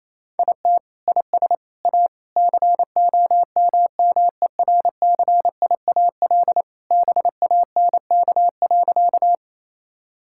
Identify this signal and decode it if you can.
Morse code, heard as a keyed tone, about 27 wpm, 700 Hz. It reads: IT IS A COMMERCIAL BANK.
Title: Sentences from Top 1000 Words - Review Part 6 - 28wpm